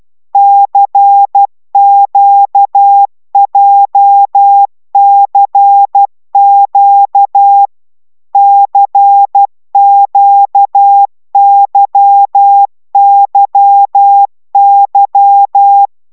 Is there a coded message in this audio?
Yes: Morse code from 0.3 to 15.9 s, heard as a keyed tone, about 12 wpm, 800 Hz. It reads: CQJCQ CQYYY